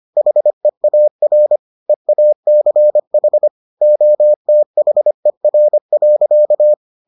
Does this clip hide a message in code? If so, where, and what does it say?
Yes: Morse code from 0.2 to 6.7 s, heard as a keyed tone, about 25 wpm, 600 Hz. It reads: HEAR EACH OTHER.